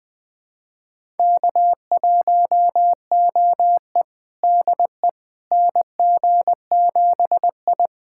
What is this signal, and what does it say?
Morse code, heard as a keyed tone, about 20 wpm, 700 Hz. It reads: K1OE DE NG7I
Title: USA Callsign DE Callsign - 4 Characters - 20wpm